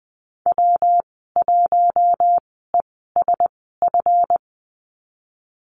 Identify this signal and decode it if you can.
Morse code, heard as a keyed tone, about 20 wpm, spaced as for 15 wpm, 700 Hz. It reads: W1ESF
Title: 193 Lesson 31g - Letter v - Call Signs 20wpm